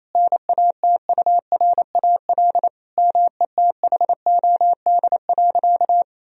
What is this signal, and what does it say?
Morse code, heard as a keyed tone, about 27 wpm, 700 Hz. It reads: NATURAL METHOD.
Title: Sentences from Top 1000 Words - Review Part 4 - 28wpm